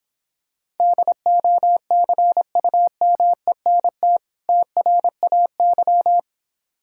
Morse code, heard as a keyed tone, about 26 wpm, 690 Hz. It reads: DOCUMENT TRAY